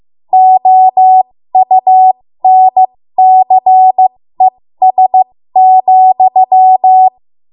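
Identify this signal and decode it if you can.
Morse code, heard as a keyed tone, about 15 wpm, spaced as for 13 wpm, 750 Hz. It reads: OUNCES,
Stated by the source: 13 wpm Morse code practice archive transmission.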